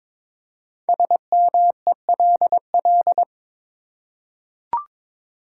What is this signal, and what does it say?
Morse code, heard as a keyed tone, about 21 wpm, 700 Hz. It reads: SMELL
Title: Five Letter Words - 22wpm